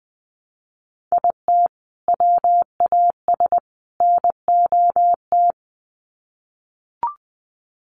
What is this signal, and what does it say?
Morse code, heard as a keyed tone, about 20 wpm, 700 Hz. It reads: IT WAS NOT
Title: Sets of 3 Words 20wpm